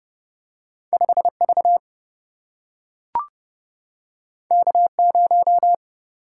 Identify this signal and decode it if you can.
Morse code, heard as a keyed tone, about 30 wpm, 700 Hz. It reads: 5 V K 0
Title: Single-Letter Single-Number 30wpm